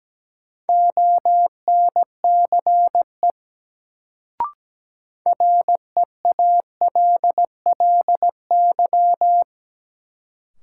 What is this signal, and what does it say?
Morse code, heard as a keyed tone, about 17 wpm, 700 Hz. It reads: ONCE REALLY
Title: Sets of 4 Words - New Word Drill - 17wpm